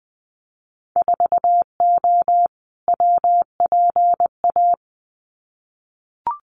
Morse code, heard as a keyed tone, about 20 wpm, 700 Hz. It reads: 4O WPA
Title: Winter Field Day Exchanges 20wpm